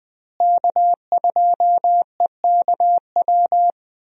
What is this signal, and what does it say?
Morse code, heard as a keyed tone, about 20 wpm, 700 Hz. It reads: K2EKW
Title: USA Call Signs - 5 Characters - Repeated 3x - 20wpm